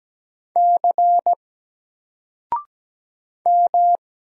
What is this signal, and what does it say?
Morse code, heard as a keyed tone, about 17 wpm, 700 Hz. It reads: C M
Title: Single Character (Letter:Number:Punctuation:Prosign) 17wpm